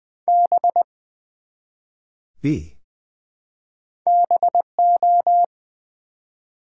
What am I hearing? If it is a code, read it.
Morse code, heard as a keyed tone, about 20 wpm, 690 Hz. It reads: B BO
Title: Build A Word - 1 Letter At A Time - 5-Letter Words - 20wpm